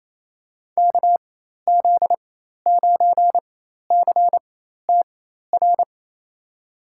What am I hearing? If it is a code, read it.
Morse code, heard as a keyed tone, about 28 wpm, spaced as for 13 wpm, 700 Hz. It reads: KZ9CTR
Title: USA Call Signs - 4x Character Spacing - 28wpm